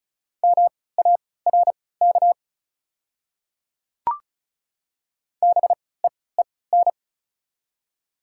Morse code, heard as a keyed tone, about 36 wpm, spaced as for 20 wpm, 690 Hz. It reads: MARK BEEN
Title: 3x Character Spacing - Top 300 Words - 35wpm